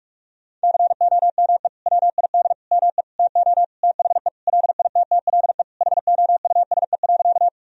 Morse code, heard as a keyed tone, about 46 wpm, 700 Hz. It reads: COME WID ME TO THE LITTLE HOUSE.
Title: Sentences from Top 200 Words and Abbreviations 45wpm